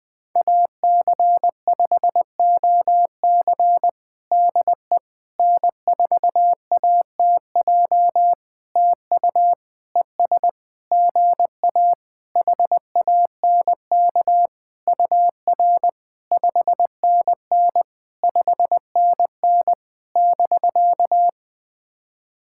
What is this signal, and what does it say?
Morse code, heard as a keyed tone, about 20 wpm, 700 Hz. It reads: AC5OC DE N4ATJ TU ES GA HANK UR 5NN 5NN <BK>
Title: POTA QSO Practice - 20wpm